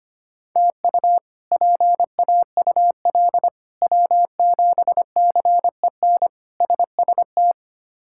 Morse code, heard as a keyed tone, about 25 wpm, 700 Hz. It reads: TU PAUL W7CEN SST
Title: K1USN SST QSO Practice - 25wpm